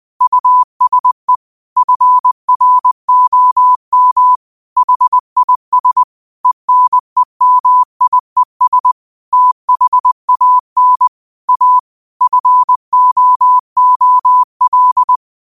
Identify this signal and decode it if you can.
Morse code, heard as a keyed tone, about 20 wpm, 1000 Hz. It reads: USE FROM HIS ENEMIES THAN A FOOL
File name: qotdfiles / 20wpm